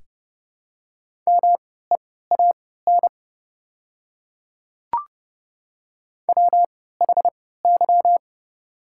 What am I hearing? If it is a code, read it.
Morse code, heard as a keyed tone, about 32 wpm, spaced as for 17 wpm, 700 Hz. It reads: MEAN WHY
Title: 3x Character Spacing - Top 200 Words - 30wpm